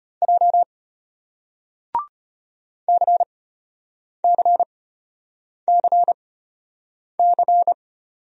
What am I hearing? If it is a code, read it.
Morse code, heard as a keyed tone, about 35 wpm, spaced as for 8 wpm, 700 Hz. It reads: J CCCC